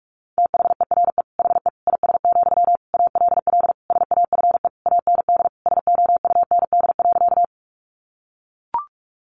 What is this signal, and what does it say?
Morse code, heard as a keyed tone, about 44 wpm, 700 Hz. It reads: THERE HE IS, ALL SAFE AND SOUND.
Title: Sentences from Top 900 Words 45wpm